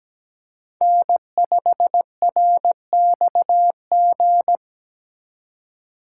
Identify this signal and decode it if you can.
Morse code, heard as a keyed tone, about 21 wpm, 700 Hz. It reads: N5RXG
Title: Speed Racing - USA 1x3 Call Signs 17wpm